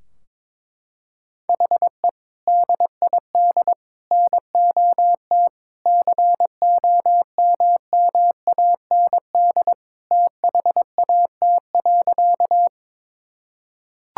Morse code, heard as a keyed tone, about 22 wpm, 700 Hz. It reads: HE DID NOT COMMAND THAT.
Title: Sentences from Top 1600 Words 22wpm